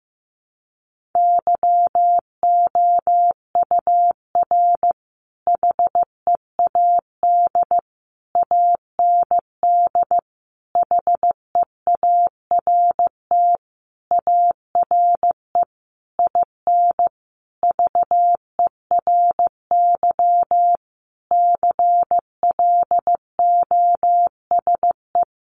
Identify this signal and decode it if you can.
Morse code, heard as a keyed tone, about 15 wpm, 700 Hz. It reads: YOUR HEAD AND HEART ARE IN VERY CLOSE
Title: Sentences from Top 1300 Words 15wpm